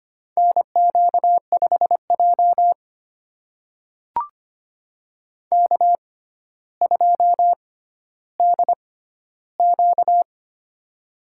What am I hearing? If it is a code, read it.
Morse code, heard as a keyed tone, about 25 wpm, spaced as for 13 wpm, 700 Hz. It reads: NQ5J K2DQ
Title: USA Call Signs - 4 Characters - 6x Character Spacing - 25wpm